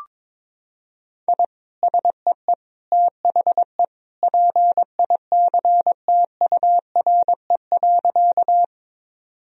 Morse code, heard as a keyed tone, about 22 wpm, 700 Hz. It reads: I SEE THE PICTURE.
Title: Sentences from Top 1000 Words - Review Part 1 - 22wpm